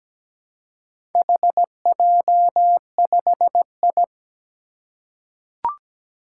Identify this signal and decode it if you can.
Morse code, heard as a keyed tone, about 17 wpm, 700 Hz. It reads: HJ5I
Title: Four Challenging Letter-Number Combinations 17wpm